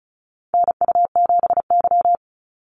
Morse code, heard as a keyed tone, about 34 wpm, 700 Hz. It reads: NU7Y
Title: NA SOTA Chaser Callsigns 35wpm